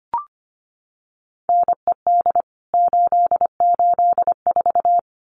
Morse code, heard as a keyed tone, about 25 wpm, 700 Hz. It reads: NED 884
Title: CWT Exchange 25wpm